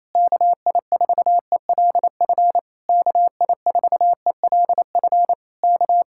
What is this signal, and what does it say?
Morse code, heard as a keyed tone, about 27 wpm, 700 Hz. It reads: KI4ELF KI4ELF K
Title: Speed-Racing Common QSO Phrases Encore - 28wpm